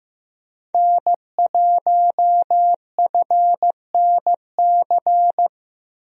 Morse code, heard as a keyed tone, about 15 wpm, 700 Hz. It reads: N1FNC
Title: USA Call Signs - 5 Characters - 15wpm